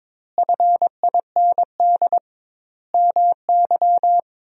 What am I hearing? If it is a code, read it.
Morse code, heard as a keyed tone, about 21 wpm, 700 Hz. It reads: FIND MY
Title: 2x Word-Spacing - Sentences from Top 200 Words - 22wpm